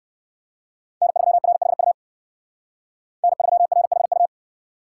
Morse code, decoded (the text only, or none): N3RHU N3RHU